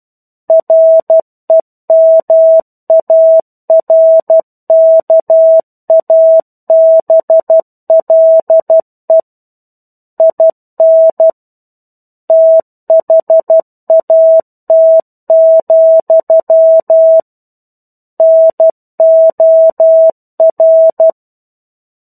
Morse code, heard as a keyed tone, about 12 wpm, 650 Hz. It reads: REMARKABLE IN THAT, NOR
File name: 01-Chapter-01 12 WPM